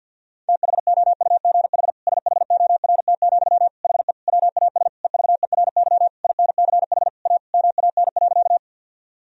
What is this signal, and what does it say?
Morse code, heard as a keyed tone, about 50 wpm, 700 Hz. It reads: THOUGH SHORT, HE WAS EVERY INCH A MAN.
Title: Sentences from Top 1000 Words - Review Part 3 - 50wpm